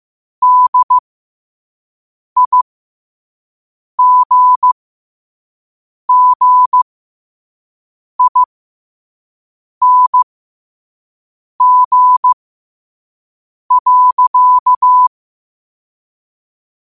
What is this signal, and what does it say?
Morse code, heard as a keyed tone, about 15 wpm, spaced as for 5 wpm, 1000 Hz. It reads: DIGGING.